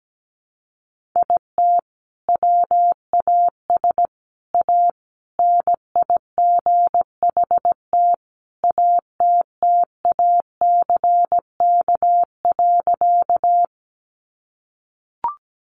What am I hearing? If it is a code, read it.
Morse code, heard as a keyed tone, about 17 wpm, 700 Hz. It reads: IT WAS A NIGHT ATTACK.
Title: Sentences from Top 1500 Words 17wpm